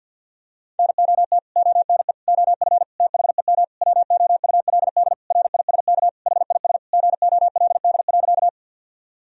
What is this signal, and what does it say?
Morse code, heard as a keyed tone, about 50 wpm, 700 Hz. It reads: NOT ONE OF THEM WOULD RISK HIS GOLD.
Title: Sentences from Top 1000 Words - Review Part 4 - 50wpm